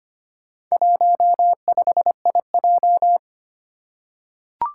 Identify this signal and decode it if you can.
Morse code, heard as a keyed tone, about 25 wpm, 700 Hz. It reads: 15IJ
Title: Four Challenging Letter-Number Combinations 25wpm